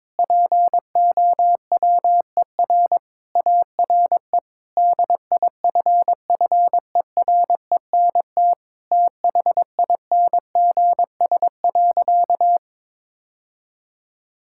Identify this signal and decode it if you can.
Morse code, heard as a keyed tone, about 22 wpm, 700 Hz. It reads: POWER ARE DIFFERENT THINGS.
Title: Sentences from Top 1000 Words - Review Part 6 - 22wpm